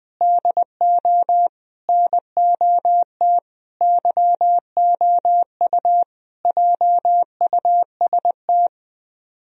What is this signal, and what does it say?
Morse code, heard as a keyed tone, about 20 wpm, 700 Hz. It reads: DO NOT YOU JUST